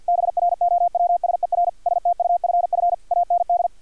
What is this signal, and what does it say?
Morse code, heard as a keyed tone, about 50 wpm, 700 Hz, with radio noise. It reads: BROWSER STUFF AND
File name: archived / CRSnet